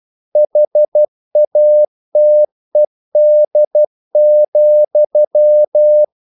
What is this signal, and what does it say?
Morse code, heard as a keyed tone, about 12 wpm, 600 Hz. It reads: HATED,
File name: CW / Practice / If12